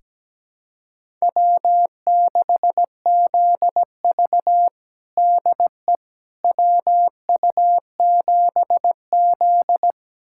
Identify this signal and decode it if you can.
Morse code, heard as a keyed tone, about 17 wpm, 700 Hz. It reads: W6ZV DE WU7Z
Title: USA Callsign DE Callsign - 4 Characters - 17wpm